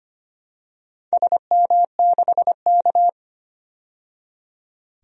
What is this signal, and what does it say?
Morse code, heard as a keyed tone, about 25 wpm, 700 Hz. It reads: SM6K
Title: DXCC-100 Call Signs - 4 Characters - 25wpm